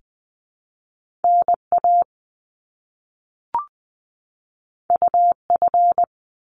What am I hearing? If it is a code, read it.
Morse code, heard as a keyed tone, about 20 wpm, 700 Hz. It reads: NA UF
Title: Top Two Letter Combinations in Words 20wpm